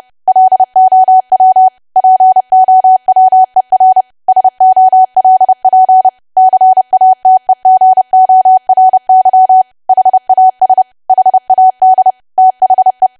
Morse code, heard as a keyed tone, about 30 wpm, 750 Hz. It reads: LOW POWER SOLP CATEGORY HAS HAD THE
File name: cwa / W1AW-Level3 / 30WPM